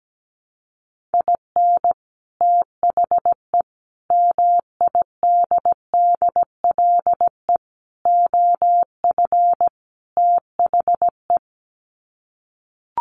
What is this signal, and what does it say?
Morse code, heard as a keyed tone, about 17 wpm, 700 Hz. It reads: IN THE MIDDLE OF THE
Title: Sets of 5 Words 17wpm